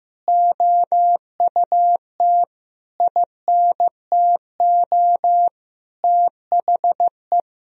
Morse code, heard as a keyed tone, about 16 wpm, 700 Hz. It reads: OUT INTO THE
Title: Sentences from Top 1000 Words - Review Part 3 - 15wpm